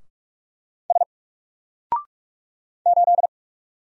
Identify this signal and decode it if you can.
Morse code, heard as a keyed tone, about 46 wpm, 700 Hz. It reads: S 8
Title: Single Character (Letter:Number:Punctuation:Prosign) 45wpm